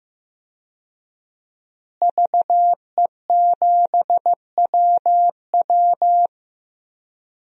Morse code, heard as a keyed tone, about 15 wpm, 700 Hz. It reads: VE7WW
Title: DXCC-100 Call Signs - 5 Characters - Repeated 3x - 15wpm